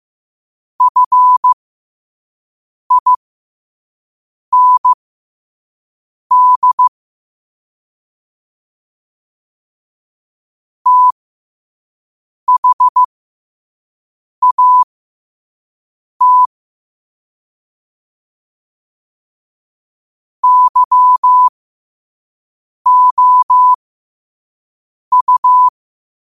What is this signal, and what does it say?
Morse code, heard as a keyed tone, about 15 wpm, spaced as for 5 wpm, 1000 Hz. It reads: FIND THAT YOU
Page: New quotes every day in morse code at 5 Words per minute.